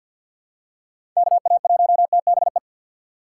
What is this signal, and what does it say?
Morse code, heard as a keyed tone, about 53 wpm, 700 Hz. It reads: KA1TBE
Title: USA Call Signs - 6 Characters - 6x Character Spacing - 50wpm